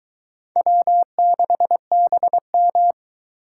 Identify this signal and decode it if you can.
Morse code, heard as a keyed tone, about 23 wpm, 700 Hz. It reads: W6BM